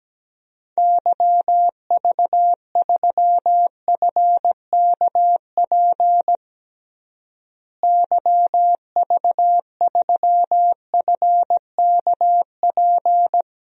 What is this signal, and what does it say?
Morse code, heard as a keyed tone, about 17 wpm, 700 Hz. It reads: YV3FKP YV3FKP
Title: DXCC-100 Call Signs - 6 Characters - Repeated 2x - 17wpm